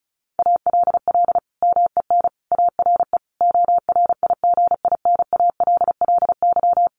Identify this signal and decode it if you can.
Morse code, heard as a keyed tone, about 34 wpm, 700 Hz. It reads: ALL MEN ARE ORIGINALLY
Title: Sentences from Top 1500 Words 35wpm